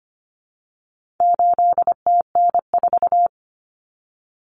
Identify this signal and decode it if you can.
Morse code, heard as a keyed tone, about 25 wpm, 700 Hz. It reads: 8TN4
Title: Common Cut Numbers - 4-Digit 25wpm